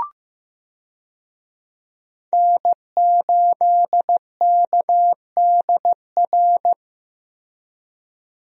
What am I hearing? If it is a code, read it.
Morse code, heard as a keyed tone, about 15 wpm, 700 Hz. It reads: N8KDR
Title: USA Call Signs - 5 Characters - Repeated 3x - 15wpm